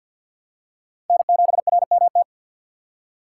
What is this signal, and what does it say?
Morse code, heard as a keyed tone, about 53 wpm, 700 Hz. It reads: N7RMT
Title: USA Call Signs - 5 Characters - 3x Character Spacing - 50wpm